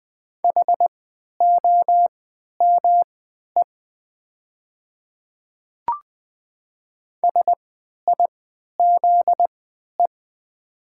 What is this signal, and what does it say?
Morse code, heard as a keyed tone, about 20 wpm, spaced as for 11 wpm, 700 Hz. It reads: HOME SIZE
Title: 3x Character Spacing - Top 500 Words - 20wpm